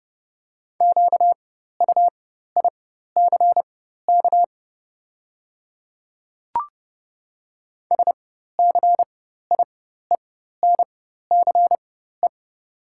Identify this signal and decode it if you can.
Morse code, heard as a keyed tone, about 30 wpm, spaced as for 14 wpm, 700 Hz. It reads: QUICK SCIENCE
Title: 4x Character Spacing - Top 500 Words - 30wpm